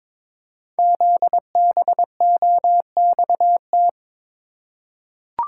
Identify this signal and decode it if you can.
Morse code, heard as a keyed tone, about 22 wpm, 700 Hz. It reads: ZBOXT
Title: Code Groups - 5 Letters - 22wpm